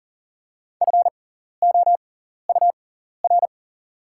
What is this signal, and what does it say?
Morse code, heard as a keyed tone, about 38 wpm, spaced as for 14 wpm, 690 Hz. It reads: FOUR
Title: 6x Character Spacing - Top 300 Words - 40wpm